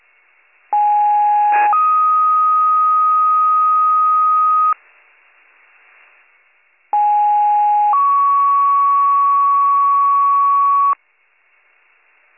Two-tone page, heard as a one-second tone, then a three-second tone, about 830 then 1250 Hz, about 830 then 1120 Hz. It is heard from 0.7 to 10.9 s.